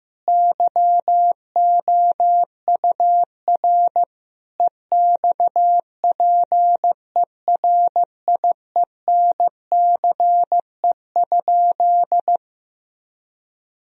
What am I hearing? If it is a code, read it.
Morse code, heard as a keyed tone, about 15 wpm, 700 Hz. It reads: YOUR EXPERIENCE?
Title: Sentences from Top 1000 Words - Review Part 4 - 15wpm